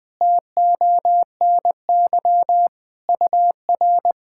Morse code, heard as a keyed tone, about 20 wpm, 700 Hz. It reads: TONY UR
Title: POTA QSO Practice - 20wpm